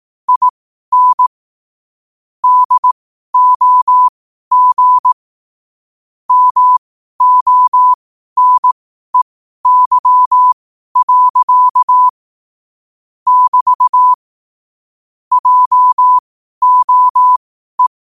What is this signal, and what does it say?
Morse code, heard as a keyed tone, about 18 wpm, spaced as for 12 wpm, 1000 Hz. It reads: IN DOG MONEY. <BT> JOE
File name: qotdfiles / 12wpm